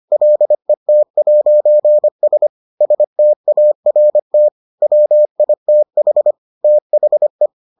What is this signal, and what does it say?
Morse code, heard as a keyed tone, about 25 wpm, 600 Hz. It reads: LET'S START WITH THE